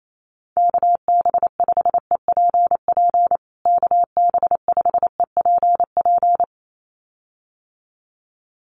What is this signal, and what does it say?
Morse code, heard as a keyed tone, about 28 wpm, 700 Hz. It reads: KB5EPP KB5EPP
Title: USA Call Signs Repeated 2x 28wpm